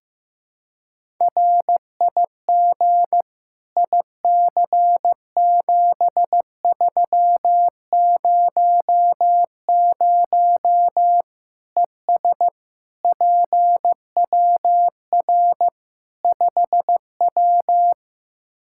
Morse code, heard as a keyed tone, about 15 wpm, 700 Hz. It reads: RIG IC7300 ES PWR 5W
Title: Practice QSO - Exchange 2 - Info - 15wpm